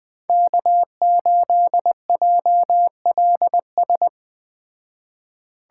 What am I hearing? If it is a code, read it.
Morse code, heard as a keyed tone, about 21 wpm, 700 Hz. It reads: K8JLS